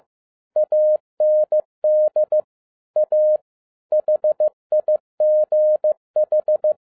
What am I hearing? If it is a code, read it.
Morse code, heard as a keyed tone, about 15 wpm, 610 Hz. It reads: AND A HIGH
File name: MY CW LIFE 1 15wpm